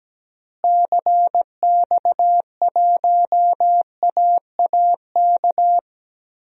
Morse code, heard as a keyed tone, about 17 wpm, 700 Hz. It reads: CX1AAK
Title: DXCC-100 Call Signs - 6 Characters - Repeated 2x - 17wpm